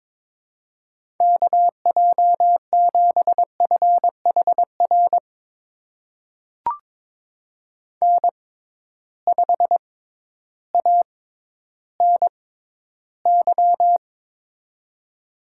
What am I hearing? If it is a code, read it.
Morse code, heard as a keyed tone, about 22 wpm, spaced as for 12 wpm, 700 Hz. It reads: KJ7FHR N5ANY
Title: USA Call Signs - 6x Character Spacing - 22wpm